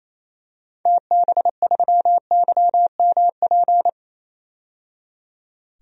Morse code, heard as a keyed tone, about 28 wpm, 700 Hz. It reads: TB3YMP
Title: DXCC-100 Call Signs - 6 Characters - 28wpm